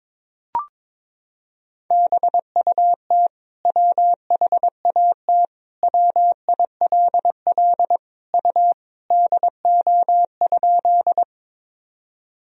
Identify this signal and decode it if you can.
Morse code, heard as a keyed tone, about 22 wpm, 700 Hz. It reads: BUT WHAT WILL U DO?
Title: Sentences from Top 100 Words and Abbreviations 22wpm